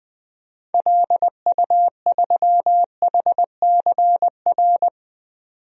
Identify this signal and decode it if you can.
Morse code, heard as a keyed tone, about 20 wpm, 700 Hz. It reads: LU3HCR
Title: DXCC-100 Call Signs - 6 Characters - 20wpm